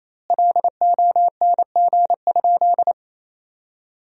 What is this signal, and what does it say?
Morse code, heard as a keyed tone, about 28 wpm, 700 Hz. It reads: LONG?
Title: 2x Word-Spacing - Sentences from Top 500 Words - 28wpm